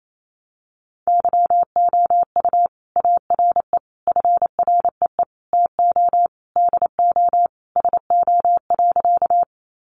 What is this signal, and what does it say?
Morse code, heard as a keyed tone, about 28 wpm, 700 Hz. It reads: YOU ARE FREE TO DO SO.
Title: Sentences from Top 500 Words 28wpm